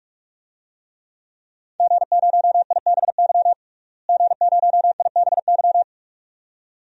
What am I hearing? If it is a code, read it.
Morse code, heard as a keyed tone, about 47 wpm, 700 Hz. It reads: G0IBY G0IBY
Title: DXCC-100 Call Signs - 5 Characters - Repeated 2x - 45wpm